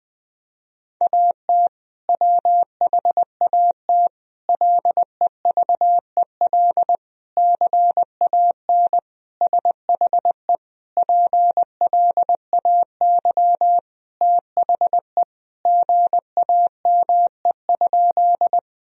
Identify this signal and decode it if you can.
Morse code, heard as a keyed tone, about 20 wpm, 700 Hz. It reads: AT WHAT LEVEL CAN SHE PLAY THE GAME?
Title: Sentences from Top 1000 Words - Review Part 4 - 20wpm